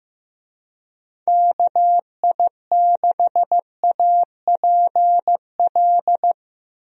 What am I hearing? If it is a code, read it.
Morse code, heard as a keyed tone, about 15 wpm, 700 Hz. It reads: KI6APL
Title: USA Call Signs - 6 Characters - Repeated 3x - 15wpm